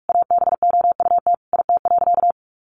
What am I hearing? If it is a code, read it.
Morse code, heard as a keyed tone, about 44 wpm, 700 Hz. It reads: ABOUT IT.
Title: Sentences from Top 1400 Words 45wpm